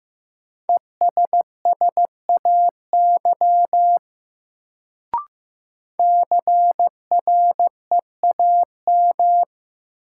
Morse code, heard as a keyed tone, about 15 wpm, 700 Hz. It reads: ESSAY CREAM
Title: Sentences from Top 1500 Words - New Word Drill Alt - 15wpm